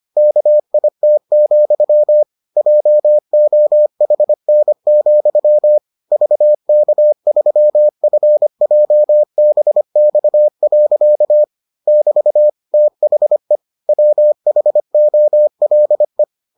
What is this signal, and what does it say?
Morse code, heard as a keyed tone, about 25 wpm, 600 Hz. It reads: KIT, JOHN, VK3FJBX. <BT> THE WHOLE